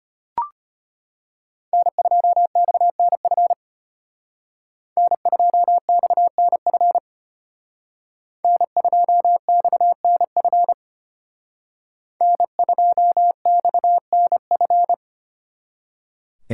Morse code, heard as a keyed tone, about 30 wpm, 700 Hz. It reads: N2XNF N2XNF N2XNF N2XNF
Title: Speed Racing - USA 1x3 Call Signs 25wpm